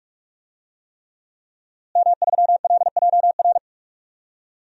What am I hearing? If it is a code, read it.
Morse code, heard as a keyed tone, about 43 wpm, 700 Hz. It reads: M3LJR
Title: DXCC-100 Call Signs - 5 Characters - Repeated 3x - 45wpm